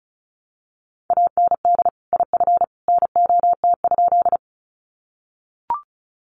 Morse code, heard as a keyed tone, about 34 wpm, 700 Hz. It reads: AND IF NOT?
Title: Sentences from Top 600 Words 35wpm